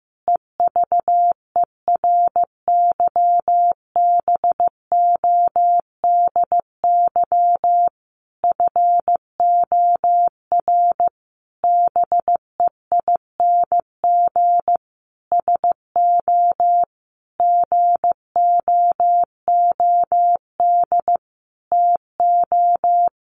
Morse code, heard as a keyed tone, about 15 wpm, 700 Hz. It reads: EVERYBODY FOR BEING SO GOOD TO
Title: Sentences from Top 1400 Words 15wpm